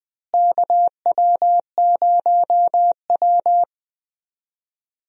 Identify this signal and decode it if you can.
Morse code, heard as a keyed tone, about 20 wpm, 700 Hz. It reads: KW0W
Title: USA Call Signs - 4 Characters - Repeated 3x - 20wpm